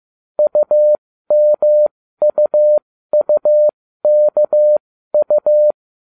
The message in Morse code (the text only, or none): UMUUKU